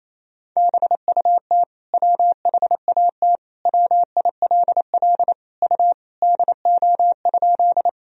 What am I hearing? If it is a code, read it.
Morse code, heard as a keyed tone, about 28 wpm, 700 Hz. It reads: BUT WHAT WILL U DO?
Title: Sentences from Top 100 Words and Abbreviations 28wpm